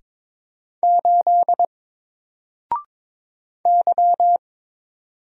Morse code, heard as a keyed tone, about 22 wpm, 700 Hz. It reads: 8 Y
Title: Single Letter-Number Alt 22wpm